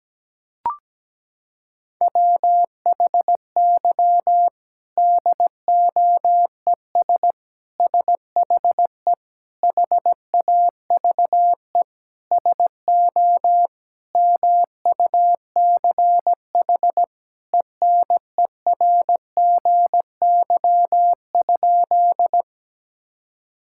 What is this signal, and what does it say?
Morse code, heard as a keyed tone, about 17 wpm, 700 Hz. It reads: WHY DOES SHE HAVE SO MUCH ENERGY?
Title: Sentences from Top 1000 Words - Review Part 6 - 17wpm